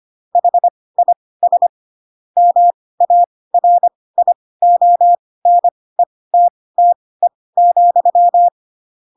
Morse code, heard as a keyed tone, about 25 wpm, spaced as for 18 wpm, 700 Hz. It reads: HIS MARIONETTE,
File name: Pinocchio ch3 12 25 ik1hge